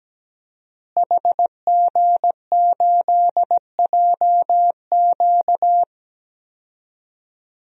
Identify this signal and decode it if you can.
Morse code, heard as a keyed tone, about 17 wpm, 700 Hz. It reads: HG8JQ
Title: DXCC-100 Call Signs - 5 Characters - Repeated 3x - 17wpm